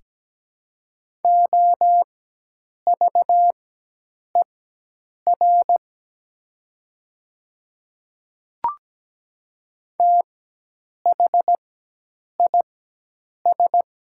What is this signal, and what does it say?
Morse code, heard as a keyed tone, about 17 wpm, spaced as for 8 wpm, 700 Hz. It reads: OVER THIS
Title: 4x Character Spacing - Top 200 Words - 17wpm